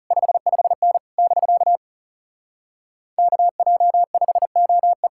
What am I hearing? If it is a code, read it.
Morse code, heard as a keyed tone, about 36 wpm, 700 Hz. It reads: 55N <BK> KJ5OE
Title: Speed-Racing Common QSO Phrases - 30wpm